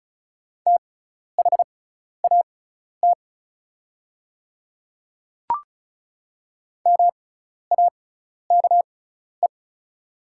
Morse code, heard as a keyed tone, about 36 wpm, spaced as for 12 wpm, 690 Hz. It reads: THAT MAKE
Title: 6x Character Spacing - Top 300 Words - 35wpm